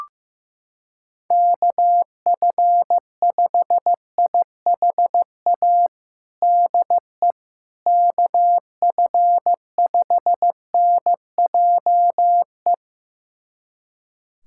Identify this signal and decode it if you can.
Morse code, heard as a keyed tone, about 15 wpm, 700 Hz. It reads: KF5IHA DE KF5NJE
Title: USA Callsign DE Callsign 15wpm